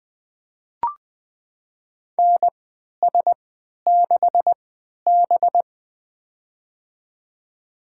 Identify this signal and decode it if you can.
Morse code, heard as a keyed tone, about 20 wpm, spaced as for 11 wpm, 700 Hz. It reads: NS6B
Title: USA Call Signs - 4 Characters - 3x Character Spacing - 20wpm